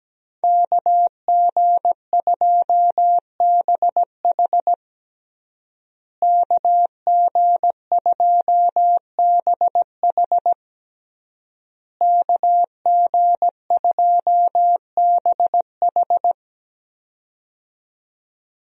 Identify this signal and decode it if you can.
Morse code, heard as a keyed tone, about 17 wpm, 700 Hz. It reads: KG2BH KG2BH KG2BH
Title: USA Call Signs - 5 Characters - Repeated 3x - 17wpm